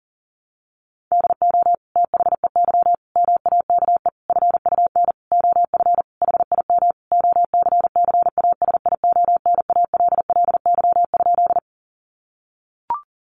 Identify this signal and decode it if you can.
Morse code, heard as a keyed tone, about 40 wpm, 700 Hz. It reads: DO THEY MAKE FUN OF HIM OCCASIONALLY?
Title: Sentences from Top 1300 Words 40wpm